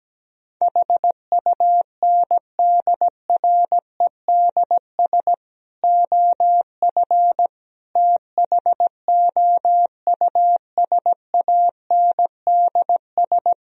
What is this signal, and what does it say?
Morse code, heard as a keyed tone, about 17 wpm, 700 Hz. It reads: HUNDREDS OF THOUSANDS
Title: Speed-Racing - Sets of 4 Words 17wpm